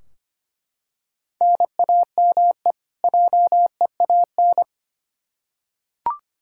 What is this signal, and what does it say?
Morse code, heard as a keyed tone, about 25 wpm, 700 Hz. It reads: NAME JEAN
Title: Practice QSO - Exchange 1 - Beginning - 25wpm